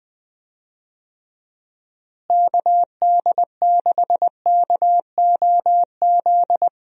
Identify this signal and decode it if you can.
Morse code, heard as a keyed tone, about 20 wpm, 700 Hz. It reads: KD6KOZ
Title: USA Call Signs - 6 Characters - Repeated 2x - 20wpm